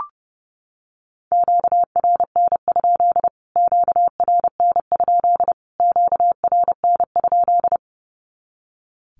Morse code, heard as a keyed tone, about 30 wpm, 700 Hz. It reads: QRN? QRN? QRN?
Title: Q-Codes Repeated 3x 30wpm